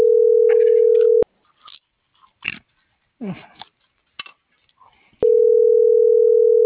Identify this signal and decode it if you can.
Ringback tone (North America), 440 + 480 Hz.
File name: CENTRAL MULTIMODAL / out josimar.cruz